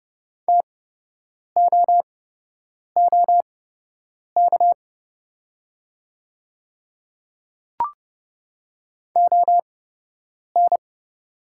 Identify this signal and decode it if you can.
Morse code, heard as a keyed tone, about 30 wpm, spaced as for 8 wpm, 700 Hz. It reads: TOOK ON
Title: 8x Character Spacing - Top 500 Words - 30wpm